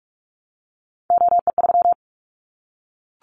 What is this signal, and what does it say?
Morse code, heard as a keyed tone, about 44 wpm, 700 Hz. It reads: OE3
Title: Callsign Prefixes for Top 25 DXCC - Repeated 3x - 45wpm